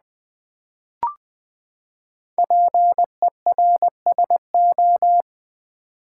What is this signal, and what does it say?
Morse code, heard as a keyed tone, about 20 wpm, 700 Hz. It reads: PERSO
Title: Build A Word - 5 Letters At A Time - 10-Letter Words - 20wpm